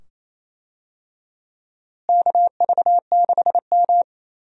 Morse code, heard as a keyed tone, about 27 wpm, 700 Hz. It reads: KV6M
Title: USA Call Signs - 4 Characters - Repeated 2x - 28wpm